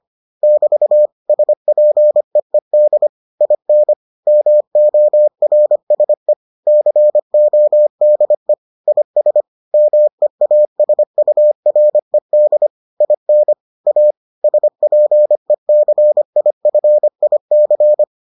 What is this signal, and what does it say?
Morse code, heard as a keyed tone, about 25 wpm, 600 Hz. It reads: <BT> SPEED IN MORSE CODE IS MEASURED IN A SPECIFIC